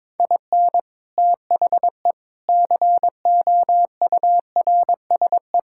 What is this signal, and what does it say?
Morse code, heard as a keyed tone, about 22 wpm, 700 Hz. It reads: IN THE COURSE
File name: practice / 22wpm / Sentences with 3-Word Phrases - 22wpm